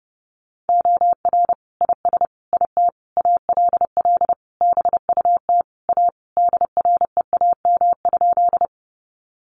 Morse code, heard as a keyed tone, about 30 wpm, 700 Hz. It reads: OR IS IT ALL BUT A DREAM?
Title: Sentences from Top 1500 Words 30wpm